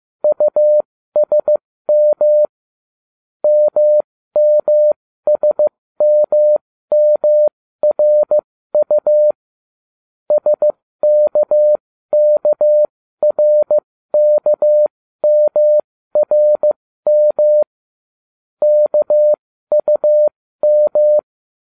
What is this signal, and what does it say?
Morse code, heard as a keyed tone, about 15 wpm, spaced as for 13 wpm, 600 Hz. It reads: USM MMSMMRU SKKRKMRM KUM